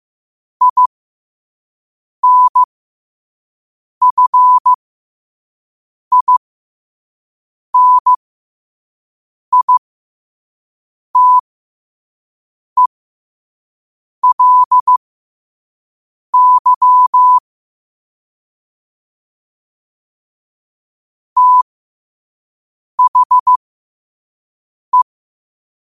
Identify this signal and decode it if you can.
Morse code, heard as a keyed tone, about 15 wpm, spaced as for 5 wpm, 1000 Hz. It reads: INFINITELY THE
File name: qotdfiles / 5wpm